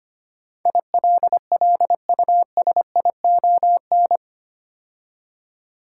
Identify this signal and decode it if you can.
Morse code, heard as a keyed tone, about 25 wpm, 700 Hz. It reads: ILLUSION
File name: 8 Letter Words - First 3 Letters Given - 25wpm